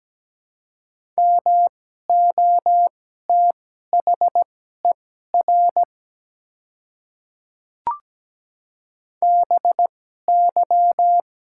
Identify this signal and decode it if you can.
Morse code, heard as a keyed tone, about 17 wpm, spaced as for 12 wpm, 700 Hz. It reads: MOTHER BY
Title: 2x Character Spacing - Top 200 Words - 17wpm